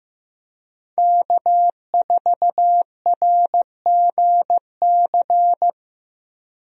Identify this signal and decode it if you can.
Morse code, heard as a keyed tone, about 15 wpm, 700 Hz. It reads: K4RGC
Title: USA Call Signs - 5 Characters - Repeated 2x - 15wpm